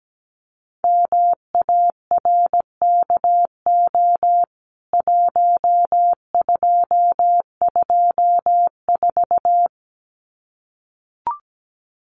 Morse code, heard as a keyed tone, about 17 wpm, 700 Hz. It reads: MARKO 1224
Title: CWT Exchange 17wpm